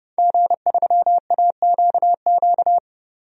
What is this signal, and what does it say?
Morse code, heard as a keyed tone, about 31 wpm, 700 Hz. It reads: G3AQQ
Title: DXCC-100 Call Signs - 5 Characters - 30wpm